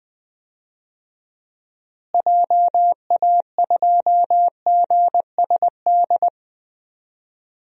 Morse code, heard as a keyed tone, about 20 wpm, 700 Hz. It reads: JA2GSD
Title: DXCC-100 Call Signs - 6 Characters - Repeated 2x - 20wpm